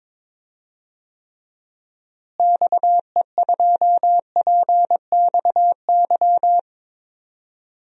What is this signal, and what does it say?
Morse code, heard as a keyed tone, about 22 wpm, 700 Hz. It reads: XE2PXY
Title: DXCC-100 Call Signs - 6 Characters - Repeated 3x - 22wpm